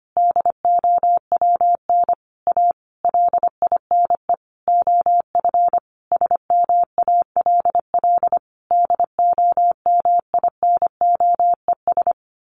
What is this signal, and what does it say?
Morse code, heard as a keyed tone, about 25 wpm, 700 Hz. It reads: DOWN A LINE OF SMALL DOMINOES
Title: A Sound Of Thunder 25wpm